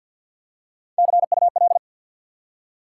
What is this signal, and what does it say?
Morse code, heard as a keyed tone, about 52 wpm, 700 Hz. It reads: CUL
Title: Top 90 QSO Elements 50wpm